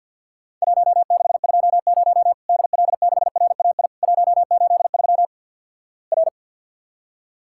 Morse code, heard as a keyed tone, about 53 wpm, 700 Hz. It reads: 1620 DL6RAI 183 R
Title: Worked All Europe DX Contest - QTC Exchange with Repeats - 50wpm